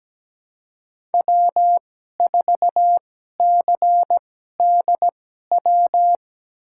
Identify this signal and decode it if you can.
Morse code, heard as a keyed tone, about 17 wpm, spaced as for 12 wpm, 700 Hz. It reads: W4CDW
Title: USA Call Signs - 5 Characters - 2x Character Spacing - 17wpm